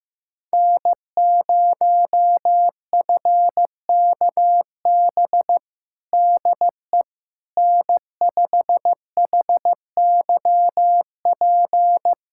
Morse code, heard as a keyed tone, about 15 wpm, 700 Hz. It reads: N0FKB DE N5HYP
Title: USA Callsign DE Callsign - 5 Characters - 15wpm